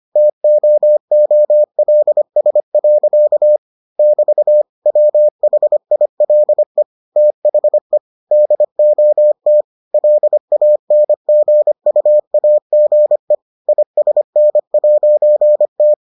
Morse code, heard as a keyed tone, about 25 wpm, 600 Hz. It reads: TOOLS. <BT> WHILE THE DOT LANGUAGE ISN'T